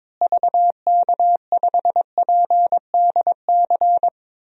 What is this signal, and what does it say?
Morse code, heard as a keyed tone, about 21 wpm, 700 Hz. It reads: VK5PDC
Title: DXCC-100 Call Signs - 6 Characters - Repeated 2x - 22wpm